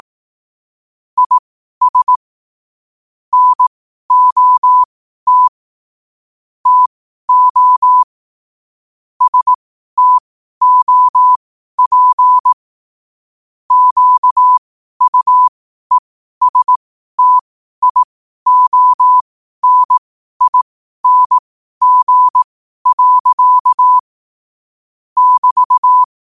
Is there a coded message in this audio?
Yes: Morse code from 1.2 to 26.0 s, heard as a keyed tone, about 18 wpm, spaced as for 12 wpm, 1000 Hz. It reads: IS NOT TO STOP QUESTIONING. <BT>